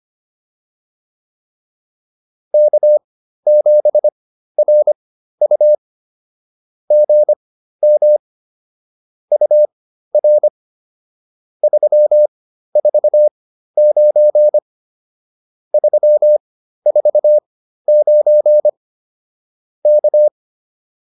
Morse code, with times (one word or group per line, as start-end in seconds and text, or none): 2.5-5.7 K7RU
6.9-8.2 GM
9.3-10.5 UR
11.6-14.6 349
15.7-18.7 349
19.9-20.3 K